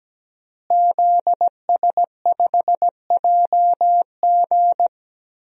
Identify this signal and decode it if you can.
Morse code, heard as a keyed tone, about 17 wpm, 700 Hz. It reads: ZS5JG
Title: DXCC-100 Call Signs - 5 Characters - Repeated 3x - 17wpm